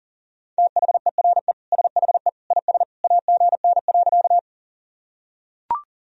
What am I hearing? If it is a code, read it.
Morse code, heard as a keyed tone, about 37 wpm, 700 Hz. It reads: THERE SHE IS AGN.
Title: Sentences from Top 200 Words and Abbreviations 40wpm